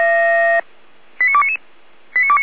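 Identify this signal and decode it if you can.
Five-tone selective call (ZVEI), heard as quick runs of five tones, tones of about 70 ms. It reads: 89299
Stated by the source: ZVEI five-tone selective call received by SDR, from the feuerwehr-orlamuende (Orlamünde fire brigade) server.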